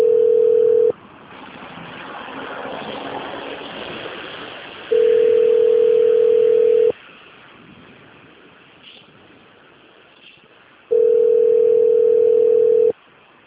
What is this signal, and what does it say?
Ringback tone (North America), 440 + 480 Hz.